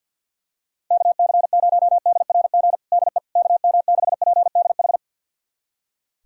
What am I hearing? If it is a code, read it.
Morse code, heard as a keyed tone, about 46 wpm, 700 Hz. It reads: KC0DRG DE KM6PDH
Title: USA Callsign DE Callsign - 6 Characters - 50wpm